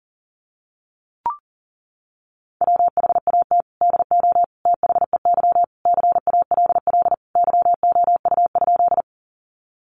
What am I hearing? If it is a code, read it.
Morse code, heard as a keyed tone, about 40 wpm, 700 Hz. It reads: WHAT DO THEY CALL YOU?
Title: Sentences from Top 100 Words 40wpm